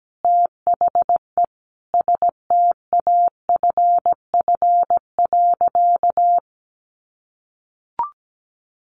Morse code, heard as a keyed tone, about 17 wpm, 700 Hz. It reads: THE STAFF.
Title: Sentences from Top 1200 Words 17wpm